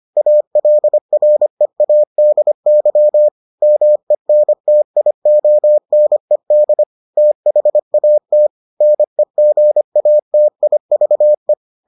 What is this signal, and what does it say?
Morse code, heard as a keyed tone, about 25 wpm, 600 Hz. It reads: ALREADY MENTIONED THAT NEGATIVE